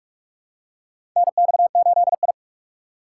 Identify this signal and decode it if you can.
Morse code, heard as a keyed tone, about 46 wpm, 700 Hz. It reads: NX8I
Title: USA Call Signs - 4 Characters - 2x Character Spacing - 45wpm